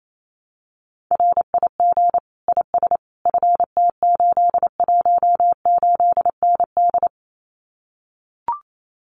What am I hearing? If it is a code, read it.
Morse code, heard as a keyed tone, about 28 wpm, 700 Hz. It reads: RIG IS FT818ND
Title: Common QSO Phrases Encore 28wpm